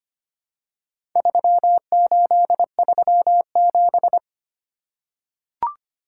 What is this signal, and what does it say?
Morse code, heard as a keyed tone, about 25 wpm, 700 Hz. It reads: 3837
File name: practice / 25wpm / Four Letter-Number Combinations - 25wpm